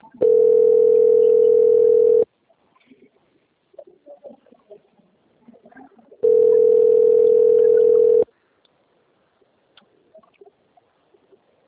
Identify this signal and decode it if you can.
Ringback tone (North America), 440 + 480 Hz.